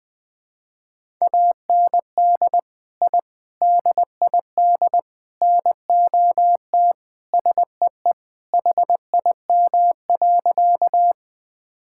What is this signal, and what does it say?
Morse code, heard as a keyed tone, about 20 wpm, 700 Hz. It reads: AND I DID NOT SEE HIM.
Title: Sentences with 4-Word Phrases - 20wpm